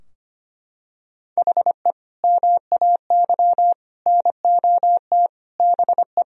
Morse code, heard as a keyed tone, about 25 wpm, 700 Hz. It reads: HE MAY NOT BE
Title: Sentences from Top 1000 Words - Review Part 2 - 25wpm